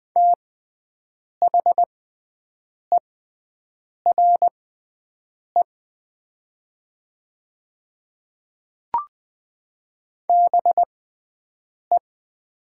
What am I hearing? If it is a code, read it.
Morse code, heard as a keyed tone, about 20 wpm, spaced as for 7 wpm, 700 Hz. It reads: THERE BE
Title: 6x Character Spacing - Top 100 Words - 20wpm